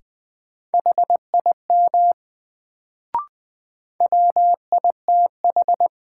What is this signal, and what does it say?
Morse code, heard as a keyed tone, about 20 wpm, 700 Hz. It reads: HIM WITH
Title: Sets of 2 Words - New Word Drill - 20wpm